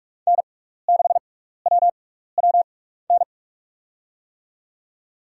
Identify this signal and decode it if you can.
Morse code, heard as a keyed tone, about 46 wpm, spaced as for 16 wpm, 700 Hz. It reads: N6WWN
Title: USA Call Signs - 6x Character Spacing - 45wpm